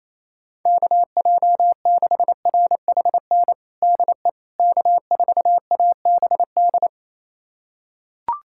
Morse code, heard as a keyed tone, about 28 wpm, 700 Hz. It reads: KJ6RHN DE K4ABD
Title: USA Callsign DE Callsign 28wpm